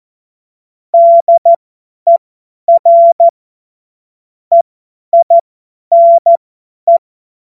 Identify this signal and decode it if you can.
Morse code, heard as a keyed tone, about 14 wpm, spaced as for 10 wpm, 680 Hz. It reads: DER EINE
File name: Morsekurs / Witze / Witz 16. Dezember 2025 50 F70